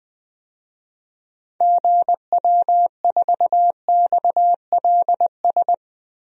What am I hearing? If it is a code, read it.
Morse code, heard as a keyed tone, about 20 wpm, 700 Hz. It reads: GW4XLS
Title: DXCC-100 Call Signs - 6 Characters - Repeated 3x - 20wpm